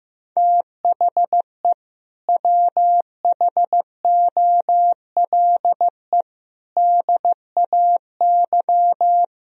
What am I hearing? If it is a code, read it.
Morse code, heard as a keyed tone, about 15 wpm, 700 Hz. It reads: THE WHOLE DAY
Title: Sentences from Top 500 Words and Abbreviations 15wpm